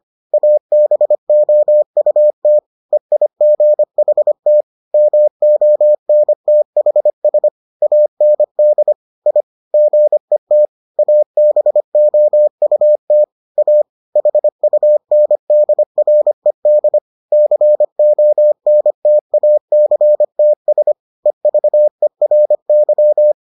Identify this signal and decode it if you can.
Morse code, heard as a keyed tone, about 25 wpm, 600 Hz. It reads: ABOUT EIGHT MONTHS AND I GET ABOUT A HUNDRED CONTACTS EVERY